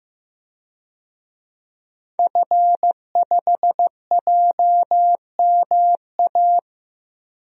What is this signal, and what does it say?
Morse code, heard as a keyed tone, about 15 wpm, 700 Hz. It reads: F5JMA
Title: DXCC-100 Call Signs - 5 Characters - 15wpm